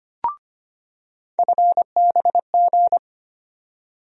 Morse code, heard as a keyed tone, about 25 wpm, 700 Hz. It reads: FBG
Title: USA Call Sign Postfixes - 3 Letters 25wpm